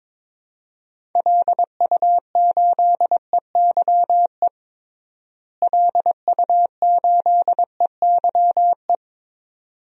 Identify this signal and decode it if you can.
Morse code, heard as a keyed tone, about 22 wpm, 700 Hz. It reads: LU8EYE LU8EYE
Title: DXCC-100 Call Signs - 6 Characters - Repeated 2x - 22wpm